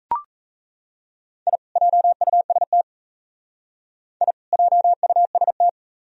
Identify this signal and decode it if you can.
Morse code, heard as a keyed tone, about 34 wpm, 700 Hz. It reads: I JUST I JUST
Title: Speed-Racing - Sets of 2 Words 28wpm